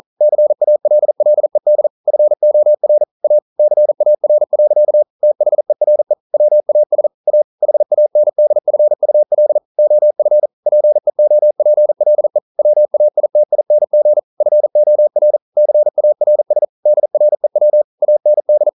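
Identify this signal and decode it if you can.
Morse code, heard as a keyed tone, about 42 wpm, 600 Hz. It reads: CALLED FOR A CAR. THERE WAS A HANDFUL OF PEOPLE WAITING FOR CARS DREW AND
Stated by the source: Morse code practice at 41 wpm, from CW med Gnister.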